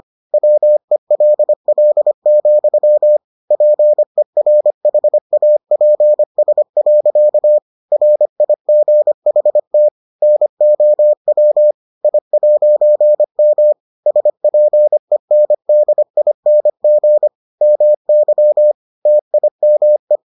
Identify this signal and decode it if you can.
Morse code, heard as a keyed tone, about 25 wpm, 600 Hz. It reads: WELL, PERHAPS. RIGHT NOW I'M SPENDING MY TIME